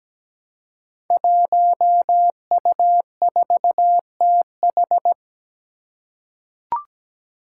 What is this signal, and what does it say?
Morse code, heard as a keyed tone, about 17 wpm, 700 Hz. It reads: 1U4TH
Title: Five Challenging Letter-Number Combinations 17wpm